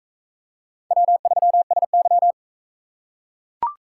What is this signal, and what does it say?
Morse code, heard as a keyed tone, about 38 wpm, 700 Hz. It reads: W3SY